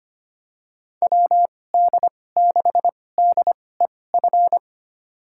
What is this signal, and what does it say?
Morse code, heard as a keyed tone, about 25 wpm, spaced as for 18 wpm, 700 Hz. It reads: WD6DEF
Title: USA Call Signs - 2x Character Spacing - 25wpm